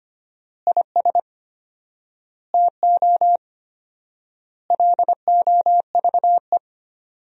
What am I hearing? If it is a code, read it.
Morse code, heard as a keyed tone, about 25 wpm, 700 Hz. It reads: IS TO LOVE
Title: 4x Word-Spacing - Sentences from Top 400 Words - 25wpm